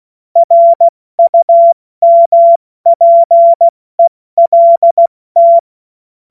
Morse code, heard as a keyed tone, about 16 wpm, spaced as for 14 wpm, 680 Hz. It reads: RUMPELT
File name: Morsekurs / Witze / Witz 21. Juni 2022 70 F80